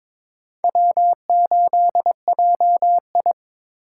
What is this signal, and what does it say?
Morse code, heard as a keyed tone, about 22 wpm, 700 Hz. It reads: W8JI
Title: USA Call Signs - 4 Characters - 2x Character Spacing - 22wpm